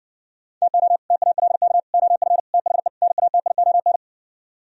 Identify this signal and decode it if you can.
Morse code, heard as a keyed tone, about 60 wpm, 700 Hz. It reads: TO TALK OF THE NATION